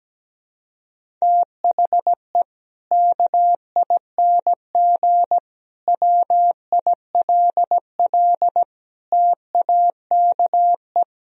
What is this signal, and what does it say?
Morse code, heard as a keyed tone, about 17 wpm, 700 Hz. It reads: THE KING WILL TAKE
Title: Sentences from Top 1000 Words - Review Part 4 - 17wpm